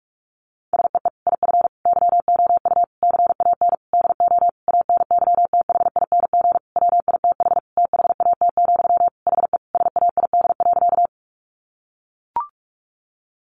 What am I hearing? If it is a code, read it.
Morse code, heard as a keyed tone, about 44 wpm, 700 Hz. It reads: SEE IF YOU CAN DO ANYTHING WITH THAT, HE SAID.
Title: Sentences from Top 1000 Words 45wpm